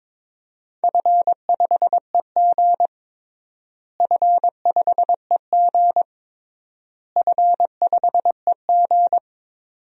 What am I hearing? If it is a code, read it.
Morse code, heard as a keyed tone, about 22 wpm, 700 Hz. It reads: F5EG F5EG F5EG
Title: DXCC-100 Call Signs - 4 Characters - Repeated 3x - 22wpm